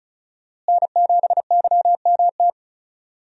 Morse code, heard as a keyed tone, about 34 wpm, 700 Hz. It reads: N7YMT
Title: USA Call Signs - 5 Characters - Repeated 2x - 35wpm